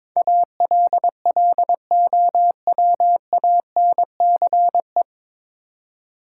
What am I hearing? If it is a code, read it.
Morse code, heard as a keyed tone, about 22 wpm, 700 Hz. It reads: ALLOWANCE
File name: 9 Letter Words - First 3 Letters Given - 22wpm